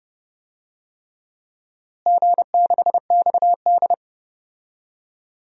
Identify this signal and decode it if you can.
Morse code, heard as a keyed tone, about 30 wpm, 700 Hz. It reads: G6XD
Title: DXCC-100 Call Signs - 4 Characters - Repeated 3x - 30wpm